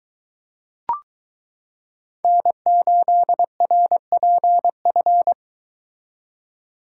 Morse code, heard as a keyed tone, about 22 wpm, 700 Hz. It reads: N8RPF